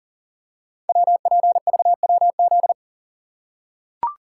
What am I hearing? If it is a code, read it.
Morse code, heard as a keyed tone, about 38 wpm, 690 Hz. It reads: WPVWZ
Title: Code Groups - 5 Letters - 40wpm